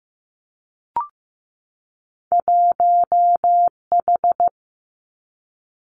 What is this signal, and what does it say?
Morse code, heard as a keyed tone, about 15 wpm, 700 Hz. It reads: 1H
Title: Two Challenging Letter-Number Combinations 15wpm